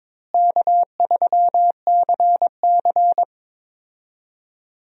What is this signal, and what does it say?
Morse code, heard as a keyed tone, about 22 wpm, 700 Hz. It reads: K3CC
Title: USA Call Signs - 4 Characters - Repeated 2x - 22wpm